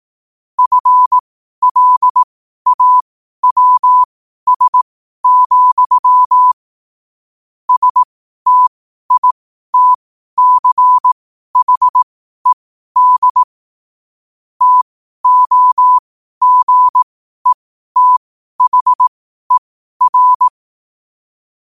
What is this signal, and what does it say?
Morse code, heard as a keyed tone, about 18 wpm, spaced as for 12 wpm, 1000 Hz. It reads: FLAWS, STITCHED TOGETHER
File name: qotdfiles / 12wpm